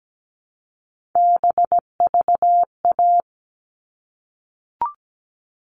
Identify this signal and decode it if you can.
Morse code, heard as a keyed tone, about 17 wpm, 700 Hz. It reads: BVA
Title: All Cut Numbers - 3-Digit 17wpm